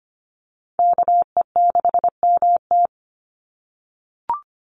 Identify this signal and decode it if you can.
Morse code, heard as a keyed tone, about 25 wpm, 700 Hz. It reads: KE6MT
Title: NA SOTA Chaser Callsigns 25wpm